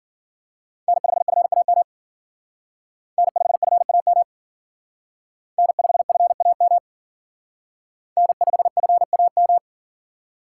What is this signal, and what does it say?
Morse code, heard as a keyed tone, about 50 wpm, 700 Hz. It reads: N5FAM N5FAM N5FAM N5FAM